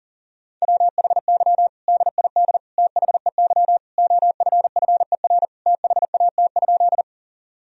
Morse code, heard as a keyed tone, about 40 wpm, 700 Hz. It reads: WHY DID THEY OFFER THAT?
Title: Sentences from Top 1000 Words - Review Part 4 - 40wpm